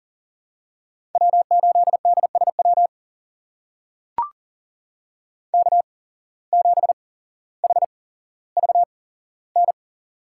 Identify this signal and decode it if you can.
Morse code, heard as a keyed tone, about 40 wpm, spaced as for 17 wpm, 700 Hz. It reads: W8DSW K7HVN
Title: USA Call Signs - 5 Characters - 8x Character Spacing - 40wpm